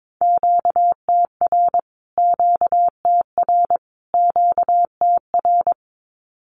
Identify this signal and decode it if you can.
Morse code, heard as a keyed tone, about 22 wpm, 700 Hz. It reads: QTR QTR QTR
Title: Q-Codes Repeated 3x 22wpm